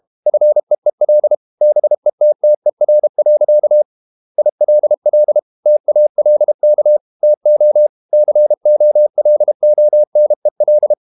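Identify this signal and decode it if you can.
Morse code, heard as a keyed tone, about 32 wpm, 600 Hz. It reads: FEEL BETTER. ILL TALK TO COLONEL
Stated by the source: Morse code practice at 32 wpm from CW med Gnister, reading Ken Follett's novel Never in English.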